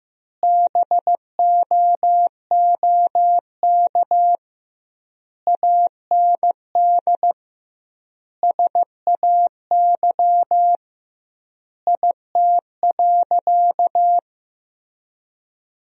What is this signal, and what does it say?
Morse code, heard as a keyed tone, about 15 wpm, 700 Hz. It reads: BOOK AND SAY IT.
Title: 2x Word-Spacing - Sentences from Top 400 Words - 15wpm